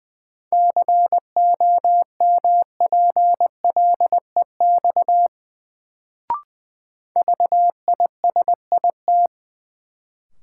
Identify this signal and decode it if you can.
Morse code, heard as a keyed tone, about 20 wpm, 700 Hz. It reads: COMPLEX VISIT
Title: Sentences from Top 900 Words - New Word Drill - 20wpm